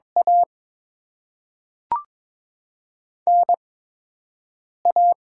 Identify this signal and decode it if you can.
Morse code, heard as a keyed tone, about 21 wpm, 700 Hz. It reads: A N A
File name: Top 20 QSO Elements - 8x Character Spacing - 22wpm